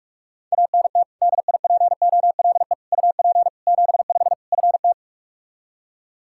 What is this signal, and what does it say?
Morse code, heard as a keyed tone, about 46 wpm, 700 Hz. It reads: ANT DIPOLE UP 75 FT
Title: Practice QSO - Exchange 2 - Info - 45wpm